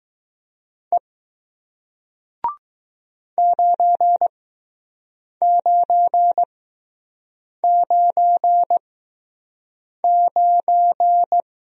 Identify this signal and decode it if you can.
Morse code, heard as a keyed tone, about 20 wpm, spaced as for 6 wpm, 700 Hz. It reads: E 9999